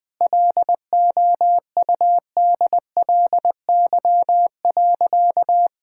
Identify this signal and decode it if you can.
Morse code, heard as a keyed tone, about 20 wpm, 700 Hz. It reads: LOUDLY.
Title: Sentences with 2-Word Phrases - 20wpm